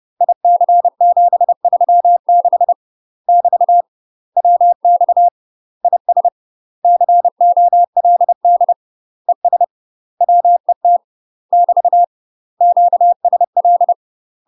Morse code, heard as a keyed tone, about 30 wpm, 700 Hz. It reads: IC736 <BT> WX IS COLD ES WET <BT> QSL
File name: UK MASTER QSOs TRAINING / 30WPM / UK MASTER QSOs 0012